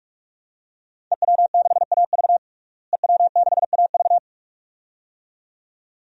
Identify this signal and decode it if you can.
Morse code, heard as a keyed tone, about 46 wpm, 700 Hz. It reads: EW6AV EW6AV
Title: DXCC-100 Call Signs - 5 Characters - Repeated 2x - 45wpm